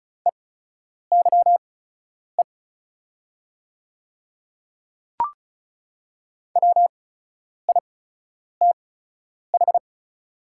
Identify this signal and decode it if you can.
Morse code, heard as a keyed tone, about 36 wpm, spaced as for 10 wpm, 700 Hz. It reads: EYE WITH
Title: 8x Character Spacing - Top 300 Words - 35wpm